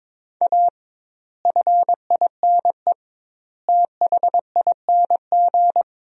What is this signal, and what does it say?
Morse code, heard as a keyed tone, about 22 wpm, 700 Hz. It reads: A FINE THING
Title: 2x Word-Spacing - Sentences from Top 500 Words - 22wpm